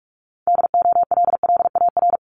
Morse code, heard as a keyed tone, about 46 wpm, 690 Hz. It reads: DOLLAR
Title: Multi-letter Phonemes 45wpm